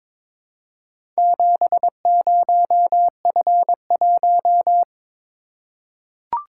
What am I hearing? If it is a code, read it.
Morse code, heard as a keyed tone, about 22 wpm, 700 Hz. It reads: 70F1
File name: practice / 22wpm / Four Letter-Number Combinations - 22wpm